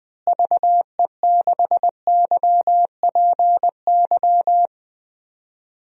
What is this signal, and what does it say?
Morse code, heard as a keyed tone, about 20 wpm, 700 Hz. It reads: VE6YPY
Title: DXCC-100 Call Signs - 6 Characters - Repeated 2x - 20wpm